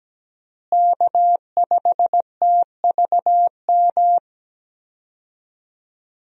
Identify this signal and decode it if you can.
Morse code, heard as a keyed tone, about 17 wpm, 700 Hz. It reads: K5TVM
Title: USA Call Signs - 5 Characters - Repeated 2x - 17wpm